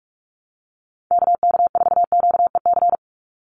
Morse code, heard as a keyed tone, about 46 wpm, 690 Hz. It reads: KK4QEC
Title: USA Call Signs 45wpm